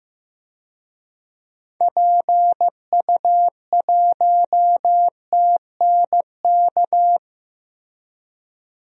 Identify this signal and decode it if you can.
Morse code, heard as a keyed tone, about 16 wpm, 700 Hz. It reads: PU1TNK
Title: DXCC-100 Call Signs - 6 Characters - Repeated 3x - 15wpm